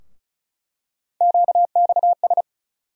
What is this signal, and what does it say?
Morse code, heard as a keyed tone, about 36 wpm, 700 Hz. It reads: QXS
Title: USA Call Sign Postfixes - 3 Letters 35wpm